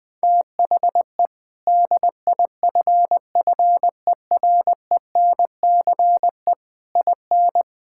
Morse code, heard as a keyed tone, about 20 wpm, 700 Hz. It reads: THE DIFFERENCE IN
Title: Sentences from Top 1600 Words 20wpm